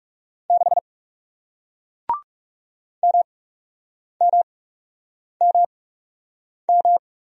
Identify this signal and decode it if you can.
Morse code, heard as a keyed tone, about 41 wpm, spaced as for 8 wpm, 700 Hz. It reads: 6 MMMM